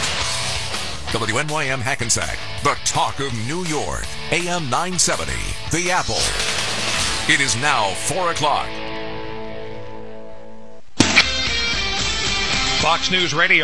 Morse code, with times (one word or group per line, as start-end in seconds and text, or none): none